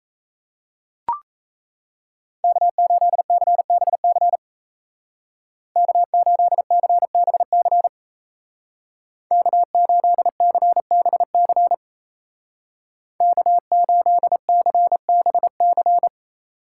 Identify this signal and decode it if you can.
Morse code, heard as a keyed tone, about 34 wpm, 700 Hz. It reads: K8CBC K8CBC K8CBC K8CBC
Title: Speed Racing - USA Call Signs 28wpm